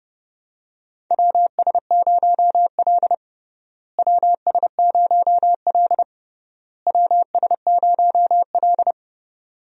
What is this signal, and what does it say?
Morse code, heard as a keyed tone, about 30 wpm, 700 Hz. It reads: WS0L WS0L WS0L
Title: USA Call Signs - 4 Characters - Repeated 3x - 30wpm